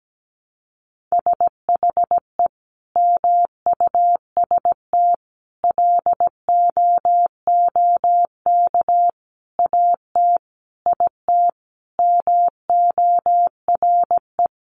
Morse code, heard as a keyed tone, about 17 wpm, 700 Hz. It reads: SHE MUST LOOK AT IT MORE
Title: Sentences from Top 1400 Words 17wpm